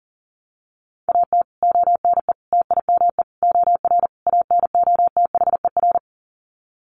Morse code, heard as a keyed tone, about 40 wpm, 700 Hz. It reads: AT ONE TIME OR ANOTHER
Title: Sets of 5 Words Encore 40wpm